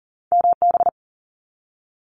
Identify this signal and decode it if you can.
Morse code, heard as a keyed tone, about 41 wpm, 700 Hz. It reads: MB
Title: Canadian Provinces and Territories 40wpm